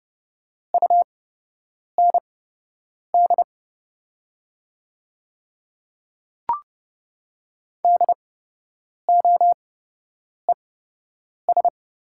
Morse code, heard as a keyed tone, about 29 wpm, spaced as for 8 wpm, 690 Hz. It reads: UND DOES
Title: 8x Character Spacing - Top 300 Words - 30wpm